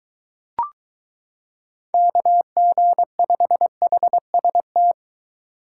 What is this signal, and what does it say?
Morse code, heard as a keyed tone, about 23 wpm, 700 Hz. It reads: KG5HST